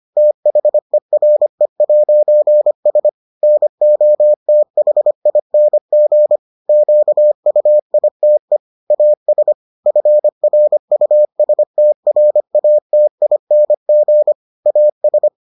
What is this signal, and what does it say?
Morse code, heard as a keyed tone, about 25 wpm, 600 Hz. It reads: THERE'S NOTHING QUITE AS FRUSTRATING AS